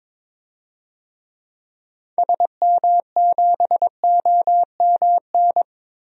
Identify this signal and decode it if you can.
Morse code, heard as a keyed tone, about 22 wpm, 700 Hz. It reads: SM7OMN
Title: DXCC-100 Call Signs - 6 Characters - Repeated 3x - 22wpm